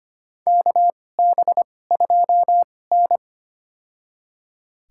Morse code, heard as a keyed tone, about 25 wpm, spaced as for 18 wpm, 700 Hz. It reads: KB2N
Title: USA Call Signs - 4 Characters - 2x Character Spacing - 25wpm